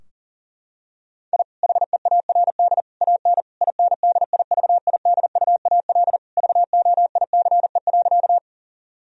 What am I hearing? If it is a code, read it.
Morse code, heard as a keyed tone, about 40 wpm, 700 Hz. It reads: I HEARD AN INDIVIDUAL VOICE.
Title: Sentences from Top 1000 Words - Review Part 4 - 40wpm